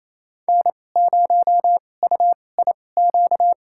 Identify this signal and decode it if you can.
Morse code, heard as a keyed tone, about 28 wpm, spaced as for 20 wpm, 700 Hz. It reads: N0UIQ
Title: USA Call Signs - 5 Characters - 2x Character Spacing - 28wpm